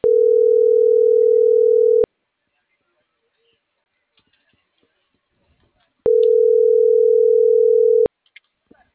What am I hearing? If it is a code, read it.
Ringback tone (North America), 440 + 480 Hz.